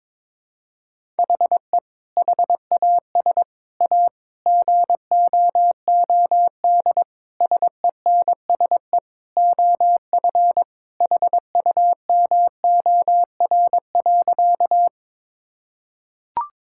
Morse code, heard as a keyed tone, about 22 wpm, 700 Hz. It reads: HE HAS A GOOD SENSE OF HUMOR.
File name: practice / 22wpm / Sentences with 2-Word Phrases - 22wpm